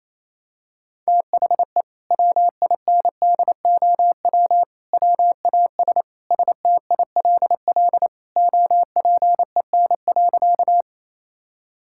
Morse code, heard as a keyed tone, about 28 wpm, 700 Hz. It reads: THE WINDOW WAS STILL OPEN.
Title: Sentences from Top 1600 Words 28wpm